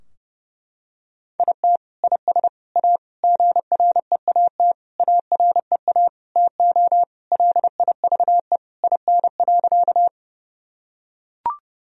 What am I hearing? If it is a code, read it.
Morse code, heard as a keyed tone, about 30 wpm, 700 Hz. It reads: IT IS A GREAT AREA TO LIVE IN.
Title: Sentences from Top 1000 Words - Review Part 2 - 30wpm